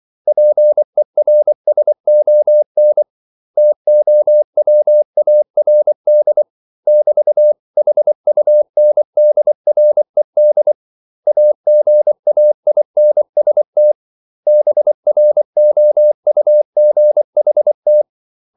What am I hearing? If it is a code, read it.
Morse code, heard as a keyed tone, about 24 wpm, 600 Hz. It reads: PERSON TOWARD <BT>HUNDRED AGAINST BROUGHT
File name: CW / Practice / 500MostCommonEnglishWords24